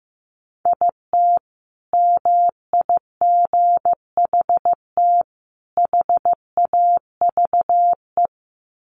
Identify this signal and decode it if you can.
Morse code, heard as a keyed tone, about 15 wpm, 700 Hz. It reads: IT MIGHT HAVE
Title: Sentences from Top 1200 Words 15wpm